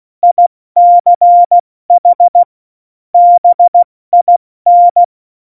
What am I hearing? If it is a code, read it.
Morse code, heard as a keyed tone, about 16 wpm, 700 Hz. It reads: ICH BIN
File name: Morsekurs / Witze / Witz 27. August 2019, 70 F80